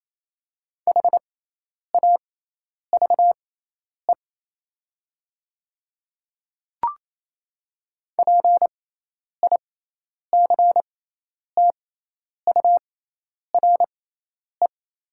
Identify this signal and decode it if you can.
Morse code, heard as a keyed tone, about 28 wpm, spaced as for 10 wpm, 700 Hz. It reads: HAVE PICTURE
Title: 6x Character Spacing - Top 300 Words - 28wpm